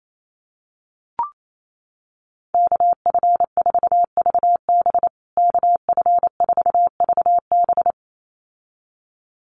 Morse code, heard as a keyed tone, about 28 wpm, 700 Hz. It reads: KF4VB KF4VB
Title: USA Call Signs Repeated 2x 28wpm